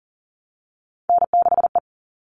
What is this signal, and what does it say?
Morse code, heard as a keyed tone, about 40 wpm, 700 Hz. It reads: N6E
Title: All Cut Numbers - 3-Digit 40wpm